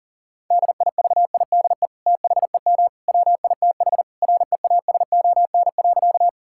Morse code, heard as a keyed tone, about 37 wpm, 700 Hz. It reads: DIVIDE THEM WITH REASON.
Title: Sentences from Top 1600 Words 40wpm